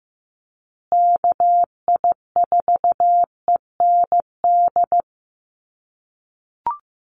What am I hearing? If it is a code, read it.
Morse code, heard as a keyed tone, about 15 wpm, 700 Hz. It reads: KI4END
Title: USA Call Signs Repeated 3x 15wpm